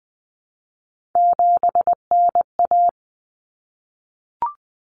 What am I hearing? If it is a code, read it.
Morse code, heard as a keyed tone, about 20 wpm, 700 Hz. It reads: 7NA
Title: Common Cut Numbers - 3-Digit 20wpm